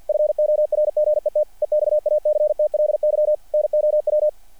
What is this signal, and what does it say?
Morse code, heard as a keyed tone, about 48 wpm, 600 Hz. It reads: FORGET EXACTLY NOW
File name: archived / CRSnet